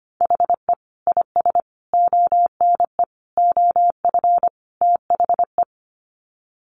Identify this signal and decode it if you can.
Morse code, heard as a keyed tone, about 25 wpm, 700 Hz. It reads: HE IS ONE OF THE
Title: Sets of 5 Words Encore 25wpm